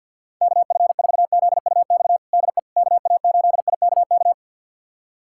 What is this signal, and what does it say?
Morse code, heard as a keyed tone, about 51 wpm, 700 Hz. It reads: KF4ZUX DE KA8IKK
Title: USA Callsign DE Callsign - 6 Characters - 50wpm